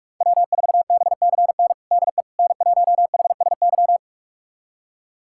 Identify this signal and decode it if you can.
Morse code, heard as a keyed tone, about 42 wpm, 700 Hz. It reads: W4BCN DE N1HSY
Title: USA Callsign DE Callsign - 5 Characters - 45wpm